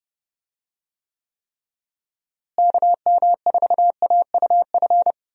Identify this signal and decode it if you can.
Morse code, heard as a keyed tone, about 30 wpm, 700 Hz. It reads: KM4AUF
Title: USA Call Signs - 6 Characters - Repeated 3x - 30wpm